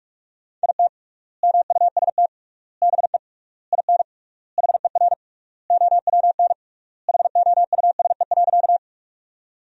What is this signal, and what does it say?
Morse code, heard as a keyed tone, about 41 wpm, 700 Hz. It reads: IT MUST BE IN HER OWN HOUSE.
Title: 3x Word-Spacing - Sentences from Top 300 Words - 45wpm